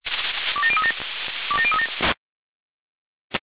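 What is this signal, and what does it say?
Five-tone selective call (ZVEI), heard as quick runs of five tones, tones of about 70 ms. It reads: 27037 27037
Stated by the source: Five-tone ZVEI selective call received by SDR.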